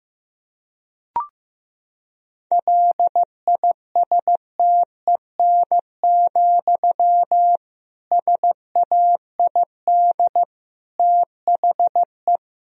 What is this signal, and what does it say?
Morse code, heard as a keyed tone, about 14 wpm, 700 Hz. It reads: LISTEN, SAID THE
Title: Sentences from Top 1000 Words - Review Part 3 - 15wpm